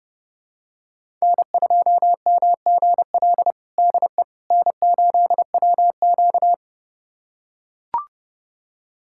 Morse code, heard as a keyed tone, about 32 wpm, 700 Hz. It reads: N2MGL DE N8WQ
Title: USA Callsign DE Callsign 30wpm